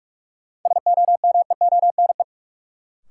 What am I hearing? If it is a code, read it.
Morse code, heard as a keyed tone, about 43 wpm, 700 Hz. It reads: SOMEONE
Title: Sentences from Top 600 Words - New Word Drill - 45wpm